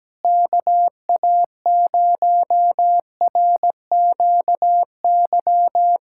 Morse code, heard as a keyed tone, about 17 wpm, 700 Hz. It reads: KA0RQY
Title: USA Callsign DE Callsign - 6 Characters - 17wpm